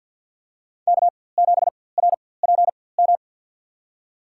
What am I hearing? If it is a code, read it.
Morse code, heard as a keyed tone, about 49 wpm, spaced as for 23 wpm, 700 Hz. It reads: K7RPM
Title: USA Call Signs - 5 Characters - 4x Character Spacing - 50wpm